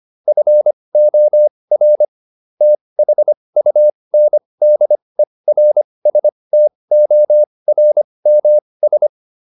Morse code, heard as a keyed tone, about 25 wpm, spaced as for 20 wpm, 600 Hz. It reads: FOR THUNDERSTORMS